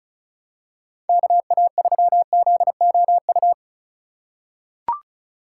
Morse code, heard as a keyed tone, about 36 wpm, 700 Hz. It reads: KA3ZOU
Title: USA Call Signs - 6 Characters - 3x Character Spacing - 35wpm